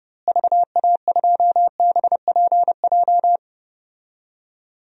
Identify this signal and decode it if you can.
Morse code, heard as a keyed tone, about 32 wpm, 700 Hz. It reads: VA2BPJ
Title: DXCC-100 Call Signs - 6 Characters - Repeated 3x - 30wpm